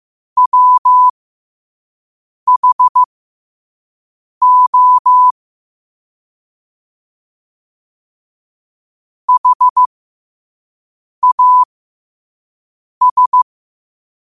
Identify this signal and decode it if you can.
Morse code, heard as a keyed tone, about 15 wpm, spaced as for 5 wpm, 1000 Hz. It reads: WHO HAS
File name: qotdfiles / 5wpm